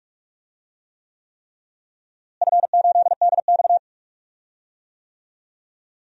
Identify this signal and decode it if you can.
Morse code, heard as a keyed tone, about 46 wpm, 700 Hz. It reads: F8DX
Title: DXCC-100 Call Signs - 4 Characters - Repeated 3x - 45wpm